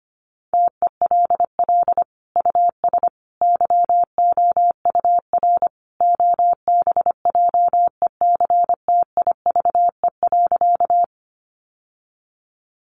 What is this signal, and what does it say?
Morse code, heard as a keyed tone, about 25 wpm, 700 Hz. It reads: TELL US YOUR OBJECTIVE.
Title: Sentences from Top 1200 Words 25wpm